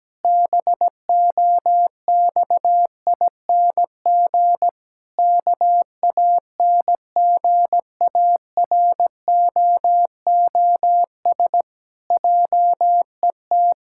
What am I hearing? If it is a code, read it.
Morse code, heard as a keyed tone, about 17 wpm, 700 Hz. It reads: BOXING KANGAROOS JET
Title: Pangram Challenge v2 - 17wpm